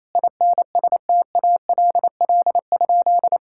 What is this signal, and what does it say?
Morse code, heard as a keyed tone, about 28 wpm, 700 Hz. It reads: INSTALL?
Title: Sentences from Top 1600 Words 28wpm